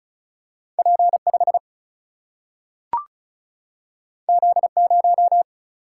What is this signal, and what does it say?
Morse code, heard as a keyed tone, about 36 wpm, 700 Hz. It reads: P5 Z0
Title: Single-Letter Single-Number 35wpm